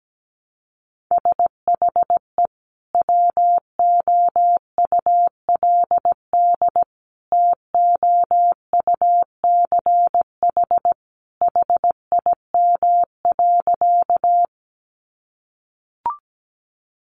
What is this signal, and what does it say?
Morse code, heard as a keyed tone, about 17 wpm, 700 Hz. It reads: SHE WOULD TOUCH HIM.
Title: Sentences from Top 1200 Words 17wpm